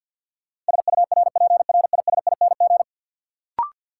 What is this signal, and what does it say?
Morse code, heard as a keyed tone, about 53 wpm, 700 Hz. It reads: SURPRISING
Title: Sets of 5 Words - New Word Drill - 50wpm